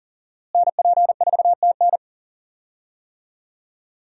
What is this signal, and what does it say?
Morse code, heard as a keyed tone, about 39 wpm, 700 Hz. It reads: NP4TN
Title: USA Call Signs - 5 Characters - Repeated 2x - 40wpm